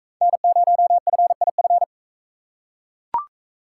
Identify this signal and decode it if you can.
Morse code, heard as a keyed tone, about 40 wpm, 700 Hz. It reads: N0FIF